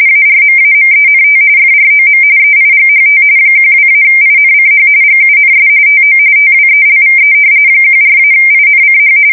RTTY, about 45.45 baud. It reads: ARMAMENT FROM EIGHT 12-INCH GUNS TO TWELVE 14-INCH
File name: fsk170 / battleshipnj